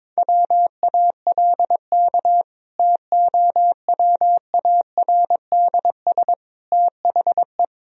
Morse code, heard as a keyed tone, about 22 wpm, 700 Hz. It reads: WALK TOWARDS THE
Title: Sentences from Top 1000 Words - Review Part 2 - 22wpm